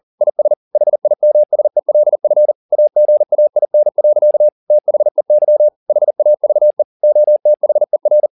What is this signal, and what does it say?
Morse code, heard as a keyed tone, about 40 wpm, 600 Hz. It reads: IS HIMSELF AGAIN. THEY HAVE OTHER